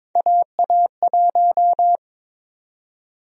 Morse code, heard as a keyed tone, about 22 wpm, 700 Hz. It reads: AA1
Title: USA Call Sign Prefixes - Repeated 2x - 22wpm